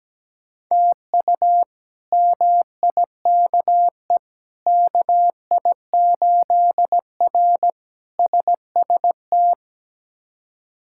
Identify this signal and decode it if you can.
Morse code, heard as a keyed tone, about 17 wpm, 700 Hz. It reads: TU MIKE KI8R SST
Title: K1USN SST QSO Practice - 17wpm